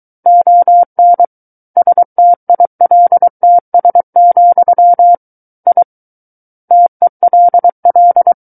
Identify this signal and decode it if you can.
Morse code, heard as a keyed tone, about 23 wpm, 700 Hz. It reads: ON STILTS, I TELL
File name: Book One / 14-Chapter 14 - 23 WPM